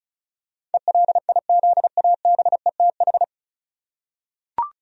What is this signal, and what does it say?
Morse code, heard as a keyed tone, about 36 wpm, 700 Hz. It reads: ELIZABETH
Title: USA Common Names - 35wpm